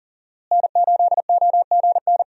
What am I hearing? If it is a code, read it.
Morse code, heard as a keyed tone, about 40 wpm, 700 Hz. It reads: N8OGN
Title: USA Call Signs - 5 Characters - Repeated 2x - 40wpm